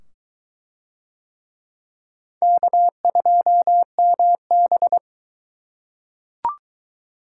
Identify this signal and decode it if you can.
Morse code, heard as a keyed tone, about 22 wpm, 700 Hz. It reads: K2MB